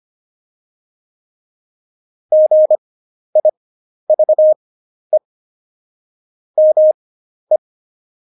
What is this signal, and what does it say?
Morse code, heard as a keyed tone, about 25 wpm, spaced as for 11 wpm, 620 Hz. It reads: GIVE ME
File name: Jokes-25-char-10-eff